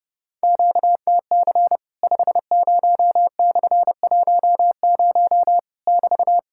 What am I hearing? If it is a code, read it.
Morse code, heard as a keyed tone, about 30 wpm, 690 Hz. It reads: QTC 50/10 <BT>
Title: Worked All Europe DX Contest - QTC Exchange - 30wpm